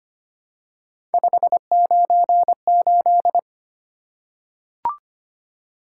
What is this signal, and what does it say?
Morse code, heard as a keyed tone, about 25 wpm, 700 Hz. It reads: 598
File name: Three Letter-Number Combinations - 25wpm